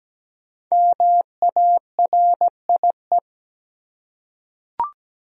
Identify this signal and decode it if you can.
Morse code, heard as a keyed tone, about 17 wpm, 700 Hz. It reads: MARIE
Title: USA Common Names - Max 5 Letters - 17wpm